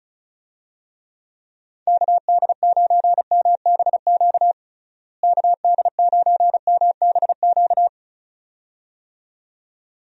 Morse code, heard as a keyed tone, about 34 wpm, 700 Hz. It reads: KD9MBQ KD9MBQ
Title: USA Call Signs - 6 Characters - Repeated 2x - 35wpm